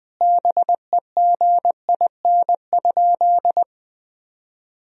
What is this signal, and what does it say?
Morse code, heard as a keyed tone, about 20 wpm, 700 Hz. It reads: BEGIN?
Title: Sentences from Top 1000 Words - Review Part 1 - 20wpm